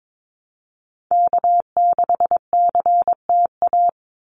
Morse code, heard as a keyed tone, about 22 wpm, 700 Hz. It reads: K6CTA
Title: Call Signs Repeated 2x - 22wpm